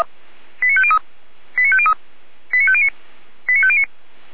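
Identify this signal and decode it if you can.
Five-tone selective call (ZVEI), heard as quick runs of five tones, tones of about 70 ms. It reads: 89503 89503 89509 89509
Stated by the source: ZVEI five-tone call received by SDR.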